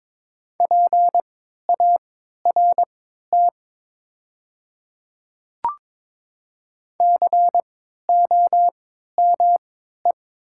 Morse code, heard as a keyed tone, about 22 wpm, spaced as for 12 wpm, 700 Hz. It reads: PART COME
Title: 3x Character Spacing - Top 200 Words - 22wpm